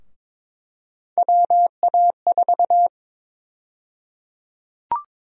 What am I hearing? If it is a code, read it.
Morse code, heard as a keyed tone, about 22 wpm, 700 Hz. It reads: WA4
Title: USA Call Sign Prefixes - Repeated 2x - 22wpm